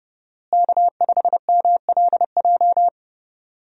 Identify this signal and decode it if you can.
Morse code, heard as a keyed tone, about 31 wpm, 700 Hz. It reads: K5MLJ
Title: USA Call Signs - 5 Characters - Repeated 2x - 30wpm